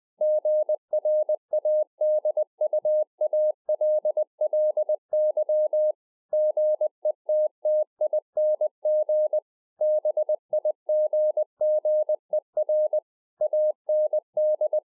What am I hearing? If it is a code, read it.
Morse code, heard as a keyed tone, about 20 wpm, 610 Hz. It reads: GRADUALLY GETTING BIGGER AND